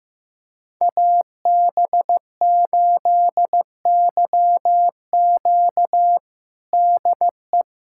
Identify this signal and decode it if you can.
Morse code, heard as a keyed tone, about 15 wpm, 700 Hz. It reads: AB8YQ DE
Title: USA Callsign DE Callsign 15wpm